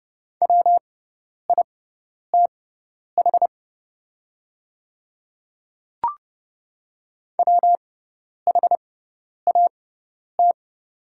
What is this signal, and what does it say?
Morse code, heard as a keyed tone, about 31 wpm, spaced as for 10 wpm, 690 Hz. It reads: WITH WHAT
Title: 6x Character Spacing - Top 200 Words - 30wpm